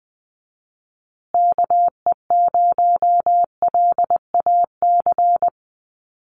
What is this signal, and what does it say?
Morse code, heard as a keyed tone, about 20 wpm, 700 Hz. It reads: KE0LAC
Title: USA Call Signs Repeated 3x 20wpm